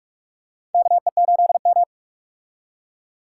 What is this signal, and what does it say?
Morse code, heard as a keyed tone, about 46 wpm, 700 Hz. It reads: KE8M
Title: K1USN SST QSO Practice - 45wpm